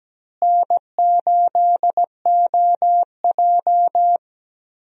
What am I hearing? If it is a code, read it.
Morse code, heard as a keyed tone, about 17 wpm, 700 Hz. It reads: N8OJ
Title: USA Call Signs - 4 Characters - Repeated 2x - 17wpm